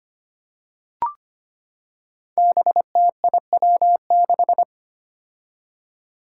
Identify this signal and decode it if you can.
Morse code, heard as a keyed tone, about 25 wpm, 700 Hz. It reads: BTIW6
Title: Five Challenging Letter-Number Combinations 25wpm